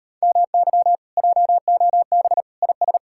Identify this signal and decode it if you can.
Morse code, heard as a keyed tone, about 33 wpm, 700 Hz. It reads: MY JOB IS